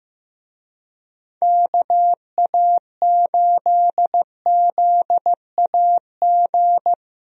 Morse code, heard as a keyed tone, about 15 wpm, 700 Hz. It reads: KA8ZAG
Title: USA Call Signs - 6 Characters - Repeated 2x - 15wpm